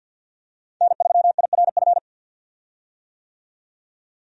Morse code, heard as a keyed tone, about 51 wpm, 700 Hz. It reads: N3IRF
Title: USA Call Signs - 5 Characters - 50wpm